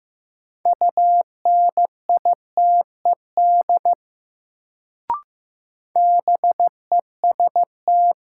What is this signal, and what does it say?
Morse code, heard as a keyed tone, about 15 wpm, 700 Hz. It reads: UNITED BEST
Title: Sets of 2 Words - New Word Drill - 15wpm